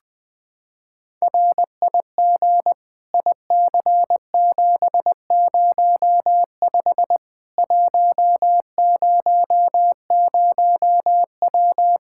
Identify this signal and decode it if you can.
Morse code, heard as a keyed tone, about 20 wpm, 700 Hz. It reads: RIG IC705 100W
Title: K1USN SST QSO Practice 20wpm